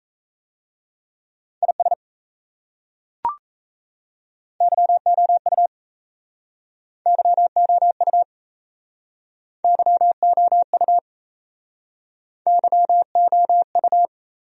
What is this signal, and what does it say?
Morse code, heard as a keyed tone, about 37 wpm, 700 Hz. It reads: IS YOU YOU YOU YOU